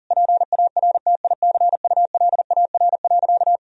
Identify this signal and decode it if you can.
Morse code, heard as a keyed tone, about 40 wpm, 700 Hz. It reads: PARTICULAR.
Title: Sentences from Top 1000 Words - Review Part 4 - 40wpm